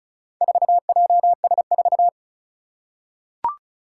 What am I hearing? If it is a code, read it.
Morse code, heard as a keyed tone, about 36 wpm, 690 Hz. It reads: 4JS4
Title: Four Challenging Letter-Number Combinations 35wpm